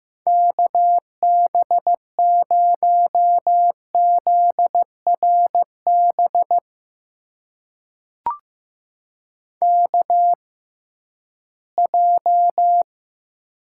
Morse code, heard as a keyed tone, about 15 wpm, spaced as for 11 wpm, 700 Hz. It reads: KB0ZRB KJ
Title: USA Call Signs - 6 Characters - 6x Character Spacing - 15wpm